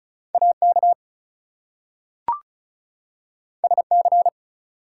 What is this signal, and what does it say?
Morse code, heard as a keyed tone, about 36 wpm, 700 Hz. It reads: AK SC
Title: US States and Canadian Provinces 35wpm